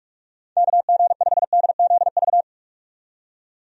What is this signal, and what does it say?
Morse code, heard as a keyed tone, about 46 wpm, 700 Hz. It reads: KG5DZV
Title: USA Call Signs - 6 Characters - 45wpm